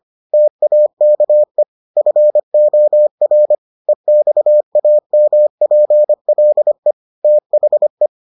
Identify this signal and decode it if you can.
Morse code, heard as a keyed tone, about 25 wpm, 600 Hz. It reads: TAKE FOR EXAMPLE THE